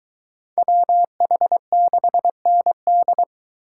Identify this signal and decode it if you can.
Morse code, heard as a keyed tone, about 22 wpm, 700 Hz. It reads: WH6ND